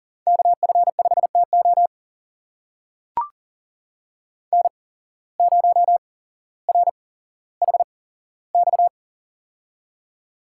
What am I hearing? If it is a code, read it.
Morse code, heard as a keyed tone, about 39 wpm, spaced as for 17 wpm, 700 Hz. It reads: KF5TO N0RHX
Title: USA Call Signs - 5 Characters - 8x Character Spacing - 40wpm